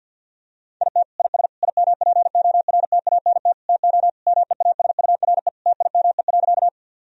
Morse code, heard as a keyed tone, about 50 wpm, 700 Hz. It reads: IT IS IMPORTANT TO MEASURE TIME.
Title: Sentences from Top 1000 Words - Review Part 2 - 50wpm